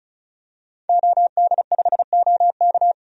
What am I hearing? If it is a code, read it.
Morse code, heard as a keyed tone, about 36 wpm, 700 Hz. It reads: OD5OK
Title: DXCC-100 Call Signs - 5 Characters - 35wpm